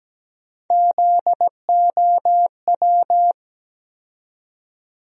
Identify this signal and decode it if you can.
Morse code, heard as a keyed tone, about 17 wpm, 700 Hz. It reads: ZOW
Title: USA Call Sign Postfixes - 3 Letters 17wpm